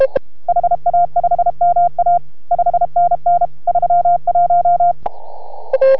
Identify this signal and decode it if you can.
Morse code, heard as a keyed tone, about 32 wpm, 690 Hz. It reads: HA5MA 5NN 31